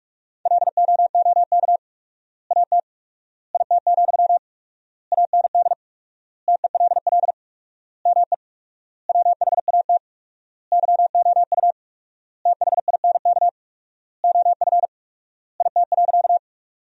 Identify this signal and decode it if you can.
Morse code, heard as a keyed tone, about 46 wpm, 700 Hz. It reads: LOOK AT IT, AND TELL ME WHAT YOU THINK OF IT.
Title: 4x Word-Spacing - Sentences from Top 200 Words - 45wpm